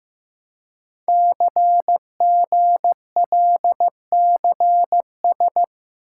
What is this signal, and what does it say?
Morse code, heard as a keyed tone, about 15 wpm, 700 Hz. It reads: CGLCS
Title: Code Groups - 5 Letters - 15wpm